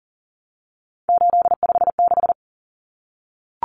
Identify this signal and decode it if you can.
Morse code, heard as a keyed tone, about 40 wpm, 700 Hz. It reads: 856
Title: Numbers - 3-Digit 40wpm